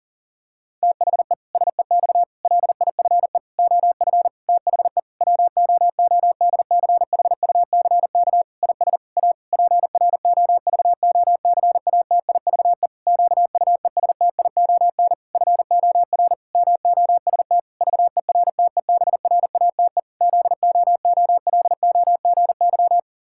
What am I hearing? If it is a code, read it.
Morse code, heard as a keyed tone, about 40 wpm, 700 Hz. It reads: THE SEX LIFE OF THE WOODCHUCK IS A PROVOCATIVE QUESTION FOR MOST VERTEBRATE ZOOLOGY
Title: Pangram Challenge v2 - 40wpm